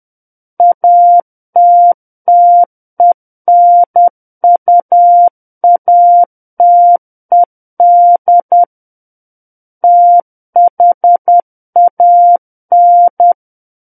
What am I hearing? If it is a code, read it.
Morse code, heard as a keyed tone, about 10 wpm, 700 Hz. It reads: ATTENUATED THAN
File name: Book One / 01-Chapter 1 - 10 WPM